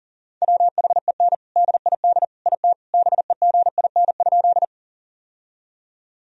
Morse code, heard as a keyed tone, about 40 wpm, 700 Hz. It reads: WHEN DID IT BEGIN?
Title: Sentences from Top 1000 Words - Review Part 1 - 40wpm